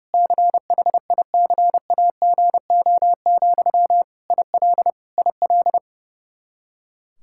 Morse code, heard as a keyed tone, about 30 wpm, 700 Hz. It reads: CHICAGO, IL IL
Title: Practice QSO - Exchange 1 - Beginning - 30wpm